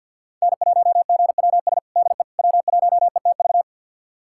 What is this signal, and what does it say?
Morse code, heard as a keyed tone, about 52 wpm, 700 Hz. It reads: N1GWS DE W1ETV
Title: USA Callsign DE Callsign - 5 Characters - 50wpm